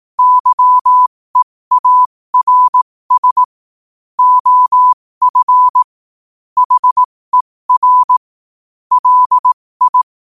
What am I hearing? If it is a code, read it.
Morse code, heard as a keyed tone, about 18 wpm, spaced as for 15 wpm, 1000 Hz. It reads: YEARS OF HER LI